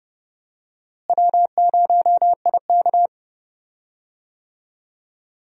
Morse code, heard as a keyed tone, about 30 wpm, 700 Hz. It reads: W0IK
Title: USA Call Signs - 4 Characters - Repeated 3x - 30wpm